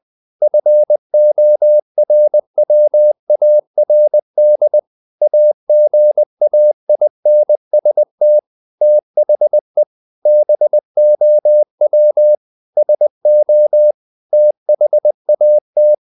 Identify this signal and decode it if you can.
Morse code, heard as a keyed tone, about 20 wpm, 600 Hz. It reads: FORWARD AGAINST THE BOW SO THAT
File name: the old man and the sea / the old man and the sea part 1 20wpm